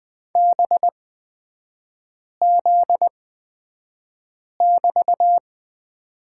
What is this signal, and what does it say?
Morse code, heard as a keyed tone, about 20 wpm, spaced as for 5 wpm, 700 Hz. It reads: BZ<BT>
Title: 250 Lesson 40e - BK - Three Characters 20wpm